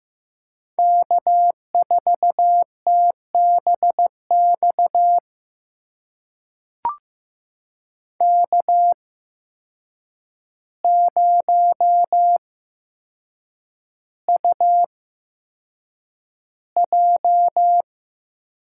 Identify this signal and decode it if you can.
Morse code, heard as a keyed tone, about 15 wpm, spaced as for 7 wpm, 700 Hz. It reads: K4TBX K0UJ
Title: USA Call Signs - 5 Characters - 8x Character Spacing - 15wpm